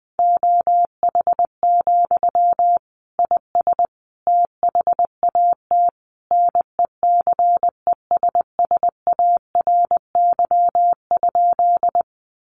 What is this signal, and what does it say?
Morse code, heard as a keyed tone, about 20 wpm, 700 Hz. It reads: OH, IS THAT NECESSARY?
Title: Sentences from Top 800 Words 20wpm